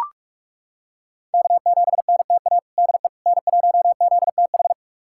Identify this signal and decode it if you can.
Morse code, heard as a keyed tone, about 41 wpm, 700 Hz. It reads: K7NTA DE N1ZTH
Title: USA Callsign DE Callsign - 5 Characters - 45wpm